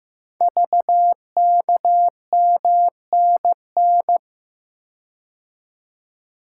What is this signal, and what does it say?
Morse code, heard as a keyed tone, about 15 wpm, 700 Hz. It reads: VKMNN
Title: Code Groups - 5 Letters - 15wpm